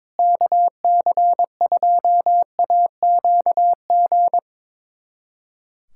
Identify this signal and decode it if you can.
Morse code, heard as a keyed tone, about 22 wpm, 700 Hz. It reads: KC2AQG
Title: USA Callsign DE Callsign - 6 Characters - 22wpm